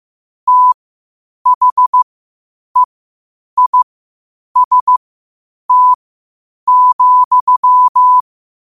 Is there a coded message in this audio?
Yes: Morse code from 0.5 to 8.2 s, heard as a keyed tone, about 15 wpm, spaced as for 8 wpm, 1000 Hz. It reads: THEIST,